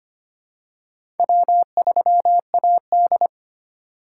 Morse code, heard as a keyed tone, about 25 wpm, 700 Hz. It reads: W3AD
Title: USA Call Signs - 4 Characters - 4x Character Spacing - 25wpm